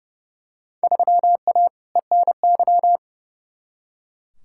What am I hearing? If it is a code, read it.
Morse code, heard as a keyed tone, about 30 wpm, 700 Hz. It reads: 3A ENY
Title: Field Day Exchanges - 30wpm